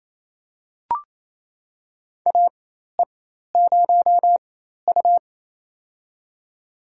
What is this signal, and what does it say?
Morse code, heard as a keyed tone, about 28 wpm, spaced as for 13 wpm, 700 Hz. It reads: AE0U
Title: USA Call Signs - 4 Characters - 4x Character Spacing - 28wpm